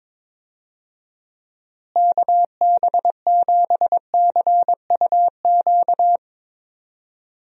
Morse code, heard as a keyed tone, about 22 wpm, 700 Hz. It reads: KB7CUQ
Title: USA Call Signs - 6 Characters - Repeated 3x - 22wpm